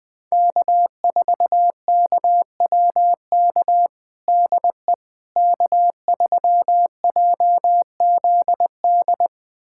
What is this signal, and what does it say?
Morse code, heard as a keyed tone, about 21 wpm, 700 Hz. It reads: K4KWK DE K3JZD
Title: USA Callsign DE Callsign - 5 Characters - 20wpm